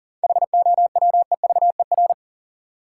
Morse code, heard as a keyed tone, about 40 wpm, 700 Hz. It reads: HOWEVER
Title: Sentences from Top 600 Words - New Word Drill - 40wpm